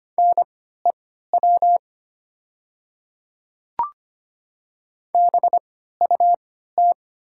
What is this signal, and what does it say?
Morse code, heard as a keyed tone, about 25 wpm, spaced as for 14 wpm, 700 Hz. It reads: NEW BUT
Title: 3x Character Spacing - Top 300 Words - 25wpm